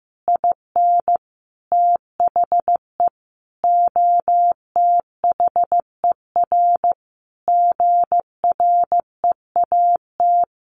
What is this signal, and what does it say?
Morse code, heard as a keyed tone, about 15 wpm, 700 Hz. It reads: IN THE OTHER GREAT
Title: Sentences from Top 1500 Words 15wpm